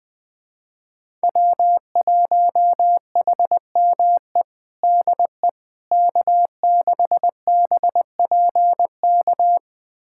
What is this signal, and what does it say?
Morse code, heard as a keyed tone, about 20 wpm, 700 Hz. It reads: W1HME DE K6BPK
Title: USA Callsign DE Callsign - 5 Characters - 20wpm